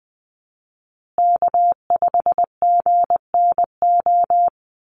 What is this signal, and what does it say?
Morse code, heard as a keyed tone, about 20 wpm, 700 Hz. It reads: K5GNO
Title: USA Call Signs Repeated 2x 20wpm